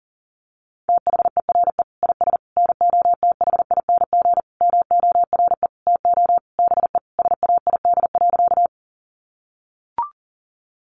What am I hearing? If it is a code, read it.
Morse code, heard as a keyed tone, about 40 wpm, 700 Hz. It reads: THERE IS NOTHING MORE TO BE SAID.
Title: Sentences from Top 500 Words 40wpm